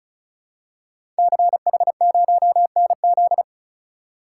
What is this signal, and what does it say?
Morse code, heard as a keyed tone, about 35 wpm, 700 Hz. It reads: CH0NZ
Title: Code Groups - 5 Letter:Numbers - 35wpm